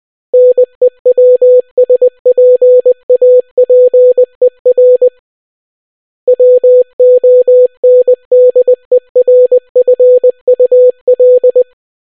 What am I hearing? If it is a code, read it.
Morse code, heard as a keyed tone, about 20 wpm, 500 Hz. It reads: NEWSPAPER WONDERFUL